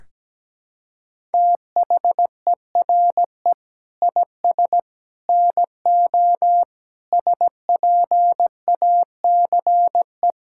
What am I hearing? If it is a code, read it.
Morse code, heard as a keyed tone, about 17 wpm, 700 Hz. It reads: THERE IS NO SPACE
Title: Sentences from Top 1000 Words - Review Part 5 - 17wpm